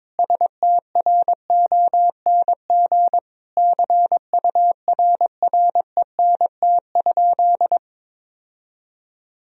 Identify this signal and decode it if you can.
Morse code, heard as a keyed tone, about 21 wpm, 700 Hz. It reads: STRONG CURRENT?
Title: Sentences from Top 1000 Words - Review Part 4 - 22wpm